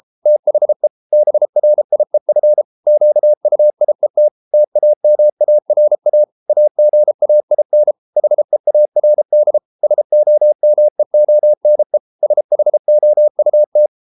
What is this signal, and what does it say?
Morse code, heard as a keyed tone, about 33 wpm, 600 Hz. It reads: THE BRIEF QUIET TAMARA AGAIN HEARD SOMEONE SHOUT